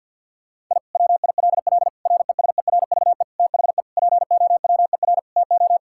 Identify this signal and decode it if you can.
Morse code, heard as a keyed tone, about 50 wpm, 700 Hz. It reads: I WILL RESERVE THE POWER TO
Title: Sentences from Top 1000 Words - Review Part 6 - 50wpm